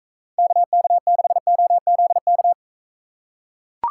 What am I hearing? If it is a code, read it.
Morse code, heard as a keyed tone, about 42 wpm, 700 Hz. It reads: KK6OZK